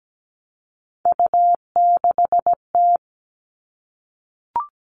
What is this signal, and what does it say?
Morse code, heard as a keyed tone, about 17 wpm, 700 Hz. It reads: U6T
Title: Three Challenging Letter-Number Combinations 17wpm